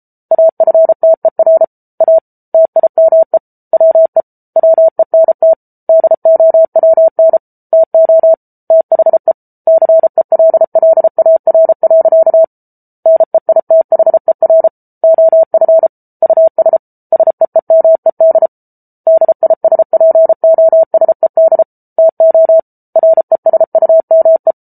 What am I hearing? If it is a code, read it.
Morse code, heard as a keyed tone, about 34 wpm, 650 Hz. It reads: AFTER A TIME WE WENT DOWN TO THE CELLAR. NEITHER OF US SEEMED DISPOSED TO RESUME